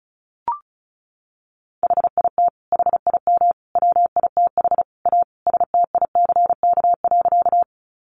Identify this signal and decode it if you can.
Morse code, heard as a keyed tone, about 34 wpm, 700 Hz. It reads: HIT HIM WITH A STICK.
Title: Sentences from Top 1200 Words 35wpm